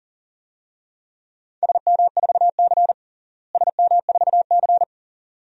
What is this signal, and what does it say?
Morse code, heard as a keyed tone, about 38 wpm, 700 Hz. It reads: SM4C SM4C
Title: DXCC-100 Call Signs - 4 Characters - Repeated 2x - 40wpm